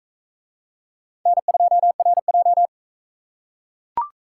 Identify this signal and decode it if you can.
Morse code, heard as a keyed tone, about 38 wpm, 700 Hz. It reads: N2RJ